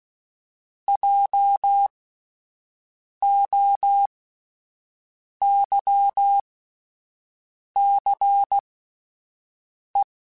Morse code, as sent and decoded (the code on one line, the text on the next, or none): .--- --- -.-- -.-. .
JOYCE